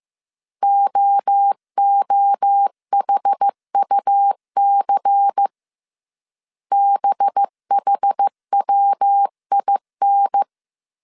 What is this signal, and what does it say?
Morse code, heard as a keyed tone, about 15 wpm, 800 Hz. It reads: OOHUC BHWIN
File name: CW / DET / 1 6 1500C S C 14wpm 2951